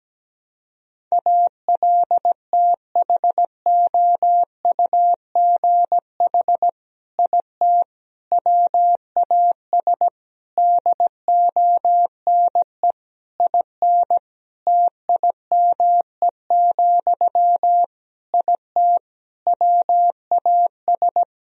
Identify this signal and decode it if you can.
Morse code, heard as a keyed tone, about 17 wpm, 700 Hz. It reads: ALTHOUGH IT WAS DONE IN TIME, IT WAS
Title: Sentences from Top 1000 Words - Review Part 3 - 17wpm